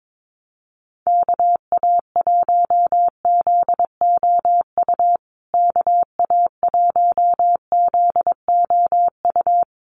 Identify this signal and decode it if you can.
Morse code, heard as a keyed tone, about 22 wpm, 700 Hz. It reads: KA1ZOU KA1ZOU
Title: USA Call Signs Repeated 2x 22wpm